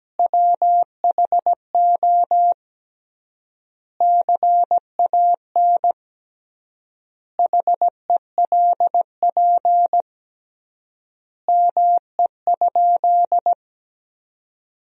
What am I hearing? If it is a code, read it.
Morse code, heard as a keyed tone, about 17 wpm, 700 Hz. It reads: WHO CAN HELP ME?
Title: 3x Word-Spacing - Sentences from Top 200 Words - 17wpm